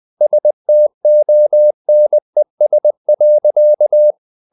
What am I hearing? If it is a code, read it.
Morse code, heard as a keyed tone, about 20 wpm, 600 Hz. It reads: STONES.
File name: CW / Practice / EnglishSayings20